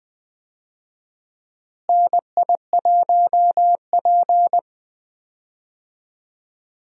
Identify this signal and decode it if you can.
Morse code, heard as a keyed tone, about 20 wpm, 700 Hz. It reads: NI1P
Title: USA Call Signs - 4 Characters - Repeated 3x - 20wpm